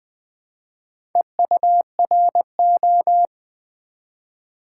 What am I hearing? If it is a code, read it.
Morse code, heard as a keyed tone, about 20 wpm, 700 Hz. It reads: EURO
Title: Build A Word - 1 Letter At A Time - 4-Letter Words - 20wpm